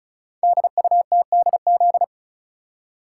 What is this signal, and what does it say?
Morse code, heard as a keyed tone, about 36 wpm, 700 Hz. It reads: DUTDZ
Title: Code Groups - 5 Letters - 35wpm